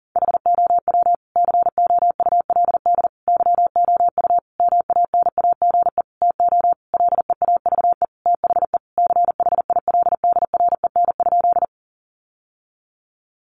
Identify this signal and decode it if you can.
Morse code, heard as a keyed tone, about 40 wpm, 700 Hz. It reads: HOW COULD YOU MANAGE TO LEAVE THE CHILDREN?
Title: Sentences from Top 1400 Words 40wpm